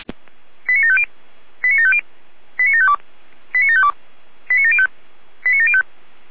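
Five-tone selective call (ZVEI), heard as quick runs of five tones, tones of about 70 ms. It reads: 89750 89750 89752 89752 89705 89705
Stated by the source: ZVEI five-tone call received on SDR.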